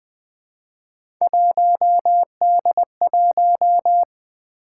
Morse code, heard as a keyed tone, about 20 wpm, 700 Hz. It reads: 1D1
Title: Three Letter-Number Combinations - 20wpm